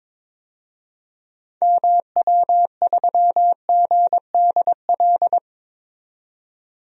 Morse code, heard as a keyed tone, about 22 wpm, 700 Hz. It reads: MW3GDL
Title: DXCC-100 Call Signs - 6 Characters - 22wpm